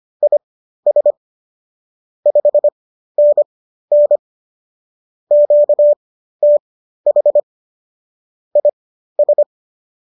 Morse code, heard as a keyed tone, about 25 wpm, spaced as for 13 wpm, 590 Hz. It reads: IS 5NN QTH IS